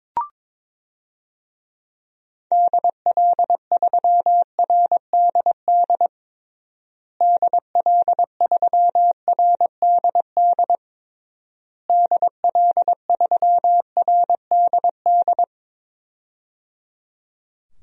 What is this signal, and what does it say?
Morse code, heard as a keyed tone, about 22 wpm, 700 Hz. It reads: DL3RDD DL3RDD DL3RDD
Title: DXCC-100 Call Signs - 6 Characters - Repeated 3x - 22wpm